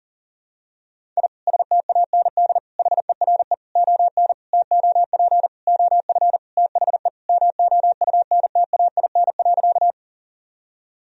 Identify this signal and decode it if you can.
Morse code, heard as a keyed tone, about 40 wpm, 700 Hz. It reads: I STAND HERE ON TOP OF THE MOUNTAIN.
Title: Sentences from Top 1000 Words - Review Part 1 - 40wpm